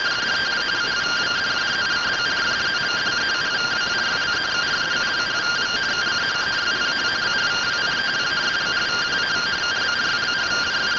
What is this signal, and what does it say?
Two-tone teleprinter signal, about 170 Hz shift.
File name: findings / CT AXIAL -572mm 2.gen-21 rtty rx